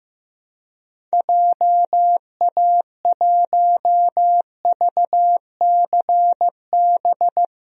Morse code, heard as a keyed tone, about 16 wpm, 700 Hz. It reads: JA1VCB
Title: DXCC-100 Call Signs - 6 Characters - 15wpm